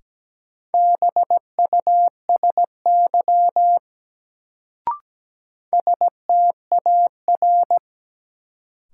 Morse code, heard as a keyed tone, about 17 wpm, 700 Hz. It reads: BUSY STAR
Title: Sentences from Top 1000 Words - Review Part 2 - New Word Drill - 17wpm